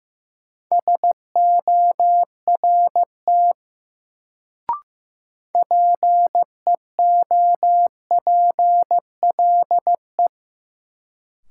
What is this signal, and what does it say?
Morse code, heard as a keyed tone, about 15 wpm, 700 Hz. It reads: SORT PEOPLE
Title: Sets of 2 Words - New Word Drill - 15wpm